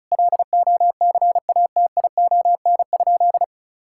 Morse code, heard as a keyed tone, about 34 wpm, 700 Hz. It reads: LOCATION?
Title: Sentences from Top 1000 Words - Review Part 6 - 35wpm